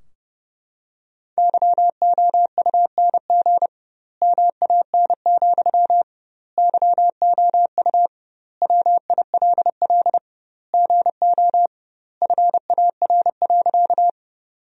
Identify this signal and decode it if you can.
Morse code, heard as a keyed tone, about 30 wpm, 700 Hz. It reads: YOUNG MAN, YOU WILL GO FAR.
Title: 2x Word-Spacing - Sentences from Top 400 Words - 30wpm